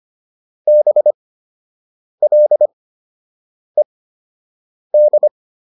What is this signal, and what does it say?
Morse code, heard as a keyed tone, about 25 wpm, spaced as for 7 wpm, 600 Hz. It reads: BLED